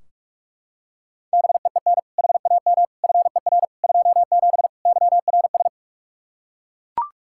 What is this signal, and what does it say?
Morse code, heard as a keyed tone, about 41 wpm, 700 Hz. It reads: BEEN HAM FER 27 YRS
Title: Practice QSO - Exchange 3 - More Info - 45wpm